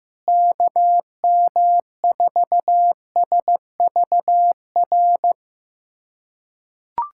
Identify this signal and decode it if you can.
Morse code, heard as a keyed tone, about 15 wpm, 700 Hz. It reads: KM4SVR
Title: USA Callsign DE Callsign - 6 Characters - 15wpm